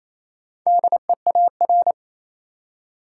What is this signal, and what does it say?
Morse code, heard as a keyed tone, about 28 wpm, 700 Hz. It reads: DEAR
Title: Four Letter Words - 28wpm